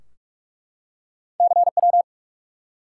Morse code, heard as a keyed tone, about 47 wpm, 700 Hz. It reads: CW
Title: Top 180 QSO Elements 45wpm